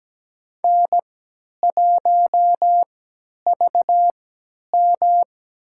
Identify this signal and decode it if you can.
Morse code, heard as a keyed tone, about 17 wpm, spaced as for 10 wpm, 700 Hz. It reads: N1VM
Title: USA Call Signs - 4 Characters - 3x Character Spacing - 17wpm